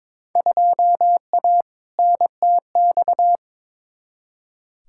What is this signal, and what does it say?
Morse code, heard as a keyed tone, about 21 wpm, 700 Hz. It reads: 2A NTX
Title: Field Day Exchanges - 22wpm